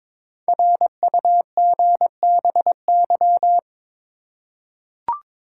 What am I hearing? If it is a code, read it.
Morse code, heard as a keyed tone, about 22 wpm, 700 Hz. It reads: RUGBY
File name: 5 Letter Words - First 3 Letters Given - 22wpm